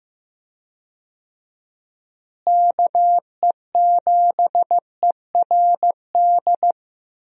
Morse code, heard as a keyed tone, about 15 wpm, 700 Hz. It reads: KE7ERD
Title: USA Call Signs - 6 Characters - Repeated 2x - 15wpm